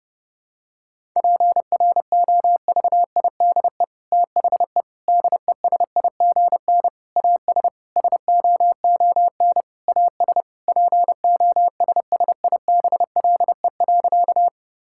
Morse code, heard as a keyed tone, about 32 wpm, 700 Hz. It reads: PROVIDE THE DESIGN AS SOON AS POSSIBLE.
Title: Sentences from Top 1000 Words - Review Part 5 - 30wpm